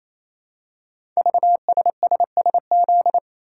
Morse code, heard as a keyed tone, about 28 wpm, 700 Hz. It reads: VSSSZ
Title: Code Groups - 5 Letters - 28wpm